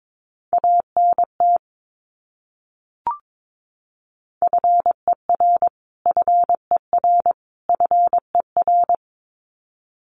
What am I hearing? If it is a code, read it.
Morse code, heard as a keyed tone, about 22 wpm, 700 Hz. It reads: ANT FER FER FER
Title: Abbreviations Repeated 3x - 22wpm